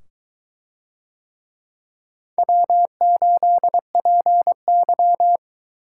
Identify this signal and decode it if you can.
Morse code, heard as a keyed tone, about 23 wpm, 700 Hz. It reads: W8PY